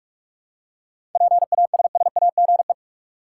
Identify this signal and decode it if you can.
Morse code, heard as a keyed tone, about 42 wpm, 700 Hz. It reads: PASSAGE
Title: Seven Letter Words - 45wpm